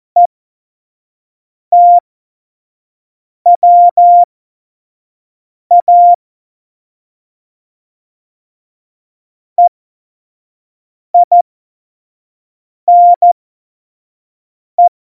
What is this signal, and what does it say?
Morse code, heard as a keyed tone, about 14 wpm, spaced as for 5 wpm, 700 Hz. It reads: ETWA EINE
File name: Morsekurs / Witze / Witz 29. Dezember 2020, 25 F70